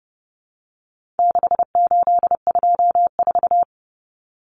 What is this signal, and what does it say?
Morse code, heard as a keyed tone, about 30 wpm, 700 Hz. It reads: 6824
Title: Common Cut Numbers - 4-Digit 30wpm